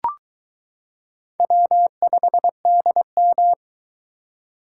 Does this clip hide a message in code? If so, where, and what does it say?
Yes: Morse code from 1.4 to 3.5 s, heard as a keyed tone, about 22 wpm, 700 Hz. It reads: W5DM